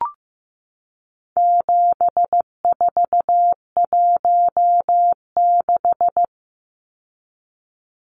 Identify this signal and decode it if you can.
Morse code, heard as a keyed tone, about 15 wpm, 700 Hz. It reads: 7416
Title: Numbers - 4-Digit 15wpm